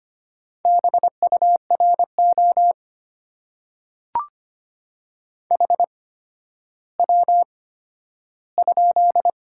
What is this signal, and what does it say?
Morse code, heard as a keyed tone, about 25 wpm, spaced as for 12 wpm, 700 Hz. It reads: BURO HW?
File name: Top 90 QSO Elements - 8x Character Spacing - 25wpm